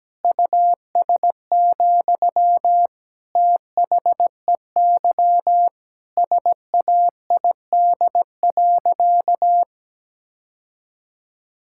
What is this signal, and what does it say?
Morse code, heard as a keyed tone, about 17 wpm, 700 Hz. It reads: US, THEY SAID.
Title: Sentences from Top 1600 Words 17wpm